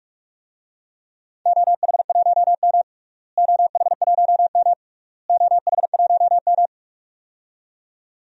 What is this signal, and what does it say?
Morse code, heard as a keyed tone, about 41 wpm, 700 Hz. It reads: OH1M OH1M OH1M
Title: DXCC-100 Call Signs - 4 Characters - Repeated 3x - 45wpm